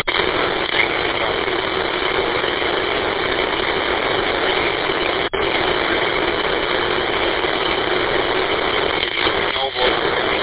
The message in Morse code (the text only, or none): none